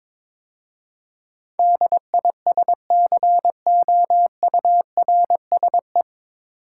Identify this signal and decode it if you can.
Morse code, heard as a keyed tone, about 22 wpm, 700 Hz. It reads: DISCOURSE
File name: 9 Letter Words - First 3 Letters Given - 22wpm